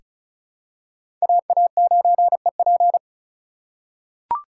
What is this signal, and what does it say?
Morse code, heard as a keyed tone, about 36 wpm, 700 Hz. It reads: AA9EP
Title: USA Call Signs - 5 Characters - 3x Character Spacing - 35wpm